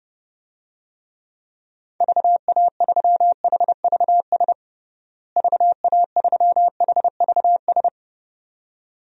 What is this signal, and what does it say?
Morse code, heard as a keyed tone, about 32 wpm, 700 Hz. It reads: VA3HVS VA3HVS
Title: DXCC-100 Call Signs - 6 Characters - Repeated 2x - 30wpm